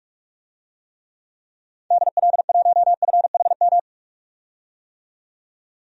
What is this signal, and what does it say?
Morse code, heard as a keyed tone, about 45 wpm, 700 Hz. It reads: DL1FHM
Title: DXCC-100 Call Signs - 6 Characters - 45wpm